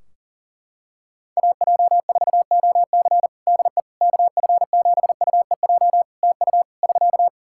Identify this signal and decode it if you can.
Morse code, heard as a keyed tone, about 40 wpm, 700 Hz. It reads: AJ4OC DE KF7UEJ TU <SK>
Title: Practice QSO - Exchange 4 - Ending - 40wpm